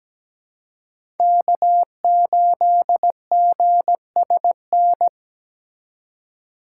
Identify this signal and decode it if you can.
Morse code, heard as a keyed tone, about 17 wpm, 700 Hz. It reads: K8GSN
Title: USA Call Signs - 5 Characters - 4x Character Spacing - 17wpm